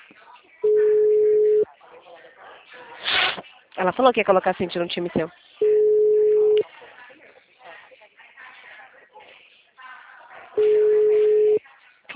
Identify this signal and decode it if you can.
Ringback tone (Europe, 425 Hz), with speech.